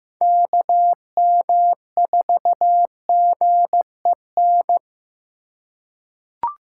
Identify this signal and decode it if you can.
Morse code, heard as a keyed tone, about 15 wpm, 700 Hz. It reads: KM4GEN
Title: USA Call Signs - 6 Characters - 4x Character Spacing - 15wpm